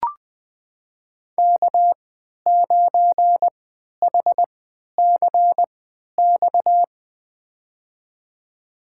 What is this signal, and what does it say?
Morse code, heard as a keyed tone, about 20 wpm, spaced as for 11 wpm, 700 Hz. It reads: K9HCX
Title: USA Call Signs - 3x Character Spacing - 20wpm